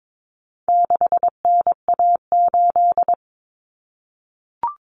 Morse code, heard as a keyed tone, about 22 wpm, 700 Hz. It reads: 6NA8
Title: Common Cut Numbers - 4-Digit 22wpm